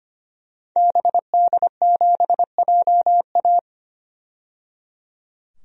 Morse code, heard as a keyed tone, about 25 wpm, 700 Hz. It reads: BD7JA
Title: DXCC-100 Call Signs - 5 Characters - 25wpm